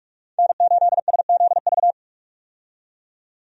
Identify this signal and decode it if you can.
Morse code, heard as a keyed tone, about 41 wpm, 700 Hz. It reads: N8SZV
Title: USA Call Signs - 5 Characters - 45wpm